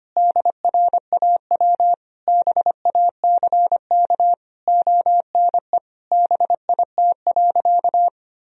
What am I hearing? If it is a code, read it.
Morse code, heard as a keyed tone, about 25 wpm, 700 Hz. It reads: DRAW BACK ONE BIT.
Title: Sentences from Top 1000 Words - Review Part 1 - 25wpm